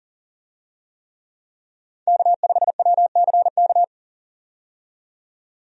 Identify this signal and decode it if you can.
Morse code, heard as a keyed tone, about 39 wpm, 700 Hz. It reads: K5WCK
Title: USA Call Signs - 5 Characters - 40wpm